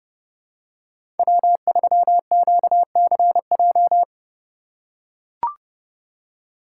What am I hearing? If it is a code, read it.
Morse code, heard as a keyed tone, about 30 wpm, 700 Hz. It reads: W3QCJ
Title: USA Call Signs - 5 Characters - 3x Character Spacing - 30wpm